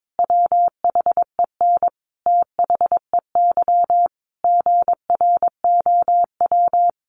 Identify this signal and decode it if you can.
Morse code, heard as a keyed tone, about 22 wpm, 700 Hz. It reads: WHEN THEY GROW
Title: Sentences from Top 1300 Words 22wpm